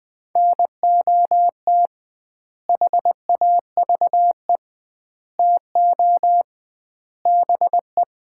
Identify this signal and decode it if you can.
Morse code, heard as a keyed tone, about 20 wpm, 700 Hz. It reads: NOT HAVE TO BE
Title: 2x Word-Spacing - Sets of 4 Words - 20wpm